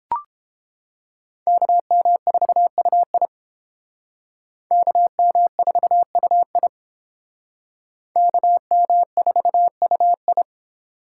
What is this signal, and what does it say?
Morse code, heard as a keyed tone, about 30 wpm, 700 Hz. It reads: KM4UI KM4UI KM4UI